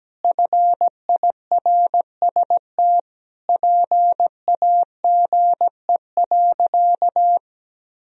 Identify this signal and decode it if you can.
Morse code, heard as a keyed tone, about 17 wpm, 700 Hz. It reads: FIRST PAGE.
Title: Sentences from Top 1000 Words - Review Part 1 - 17wpm